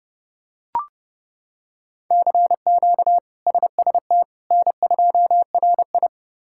Morse code, heard as a keyed tone, about 32 wpm, 700 Hz. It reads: CQ SST N2RI
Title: K1USN SST QSO Practice - 30wpm